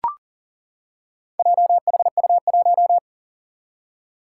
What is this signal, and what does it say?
Morse code, heard as a keyed tone, about 39 wpm, 700 Hz. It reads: JHU1
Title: Four Challenging Letter-Number Combinations 40wpm